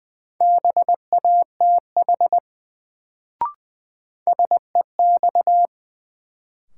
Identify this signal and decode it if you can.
Morse code, heard as a keyed tone, about 20 wpm, 700 Hz. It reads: BATH SEX
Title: Sentences from Top 1600 Words - New Word Drill - 20wpm